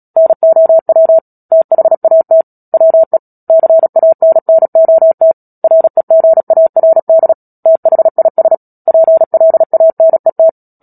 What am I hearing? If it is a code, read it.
Morse code, heard as a keyed tone, about 37 wpm, 650 Hz. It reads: NOW THAT WE CANNOT REGARD THIS PLANET
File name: Book Two / 10-Chapter 10 - Epilogue - 36 WPM